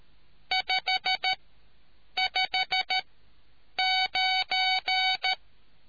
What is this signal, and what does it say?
Morse code, heard as a keyed tone, about 13 wpm, spaced as for 8 wpm, 750 Hz. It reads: 559